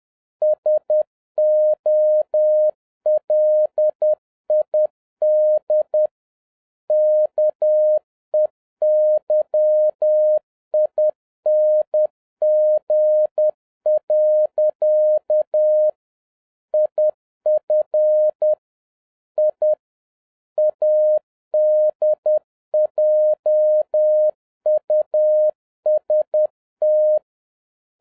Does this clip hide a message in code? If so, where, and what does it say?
Yes: Morse code from 0.4 to 27.2 s, heard as a keyed tone, about 10 wpm, 610 Hz. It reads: SOLID KEYING. IF I ADJUST